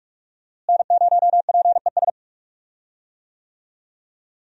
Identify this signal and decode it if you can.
Morse code, heard as a keyed tone, about 46 wpm, 700 Hz. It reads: N0PES
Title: USA Call Signs - 5 Characters - Repeated 3x - 45wpm